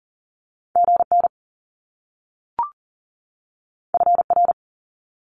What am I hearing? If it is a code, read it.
Morse code, heard as a keyed tone, about 42 wpm, 700 Hz. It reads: GN FR
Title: Top Two Letter Combinations in Words 40wpm